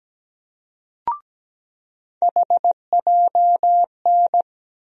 Morse code, heard as a keyed tone, about 17 wpm, 700 Hz. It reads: HJN
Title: USA Call Sign Postfixes - 3 Letters 17wpm